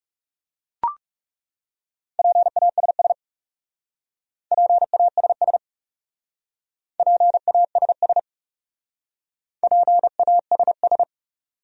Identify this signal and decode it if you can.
Morse code, heard as a keyed tone, about 37 wpm, 700 Hz. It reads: PASS PASS PASS PASS
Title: Speed-Racing Top 500 Words 30wpm